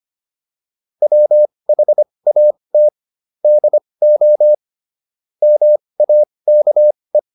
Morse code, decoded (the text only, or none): WHAT DO MAKE